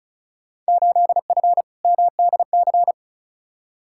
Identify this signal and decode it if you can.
Morse code, heard as a keyed tone, about 36 wpm, 700 Hz. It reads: 8F MDC
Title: Field Day Exchanges Encore - 35wpm